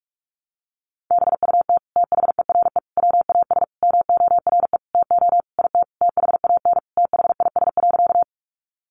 Morse code, heard as a keyed tone, about 44 wpm, 700 Hz. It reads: BUT THERE WAS MORE TO IT THAN THIS.
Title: Sentences from Top 100 Words 45wpm